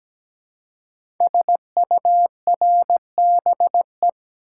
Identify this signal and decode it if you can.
Morse code, heard as a keyed tone, about 17 wpm, 700 Hz. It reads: SURBE
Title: Code Groups - 5 Letters - 17wpm